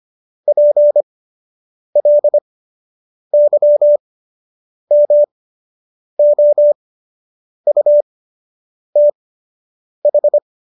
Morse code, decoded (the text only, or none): PLYMOUTH